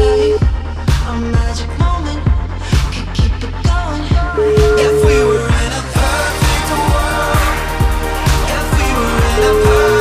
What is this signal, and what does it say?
Ringback tone (Europe, 425 Hz), with noise.